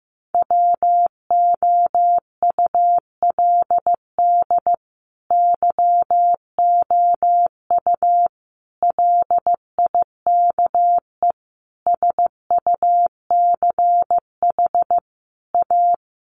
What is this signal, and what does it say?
Morse code, heard as a keyed tone, about 15 wpm, 700 Hz. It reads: WOULD YOU LIKE SUCH A
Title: Sentences from Top 1000 Words 15wpm